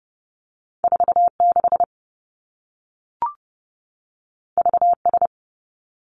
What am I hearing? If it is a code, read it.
Morse code, heard as a keyed tone, about 30 wpm, 700 Hz. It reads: 4 6 V S
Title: Two Challenging Letter-Number Combinations 30wpm